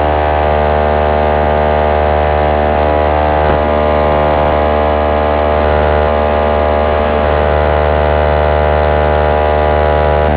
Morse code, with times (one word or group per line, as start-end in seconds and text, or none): none